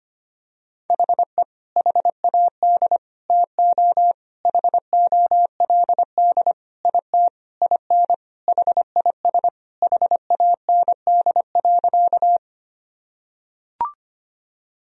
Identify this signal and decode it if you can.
Morse code, heard as a keyed tone, about 25 wpm, 700 Hz. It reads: HE HAD TO HOLD IT IN HIS HAND.
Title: Sentences from Top 1000 Words - Review Part 2 - 25wpm